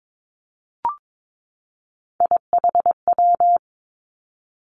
Morse code, heard as a keyed tone, about 22 wpm, 700 Hz. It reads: IHW
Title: Three Challenging Letter-Number Combinations 22wpm